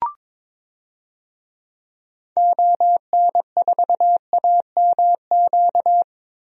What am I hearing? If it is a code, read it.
Morse code, heard as a keyed tone, about 23 wpm, 700 Hz. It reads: ON4AMQ
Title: DXCC-100 Call Signs - 6 Characters - Repeated 2x - 22wpm